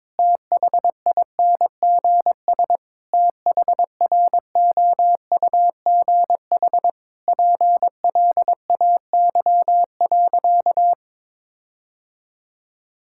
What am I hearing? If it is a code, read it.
Morse code, heard as a keyed tone, about 22 wpm, 700 Hz. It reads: THINGS THROUGH PLAY.
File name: Difficult Sentences with Hints - 22wpm